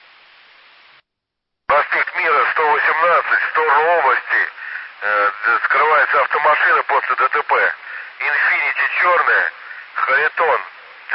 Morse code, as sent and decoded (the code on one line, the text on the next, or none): none